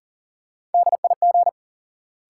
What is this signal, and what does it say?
Morse code, heard as a keyed tone, about 40 wpm, 700 Hz. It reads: DIG
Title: Sentences from Top 1600 Words - New Word Drill Alt - 40wpm